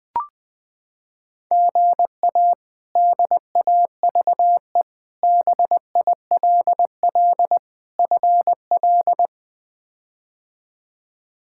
Morse code, heard as a keyed tone, about 20 wpm, 700 Hz. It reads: GA DAVE BILL FL
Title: K1USN SST QSO Practice - 20wpm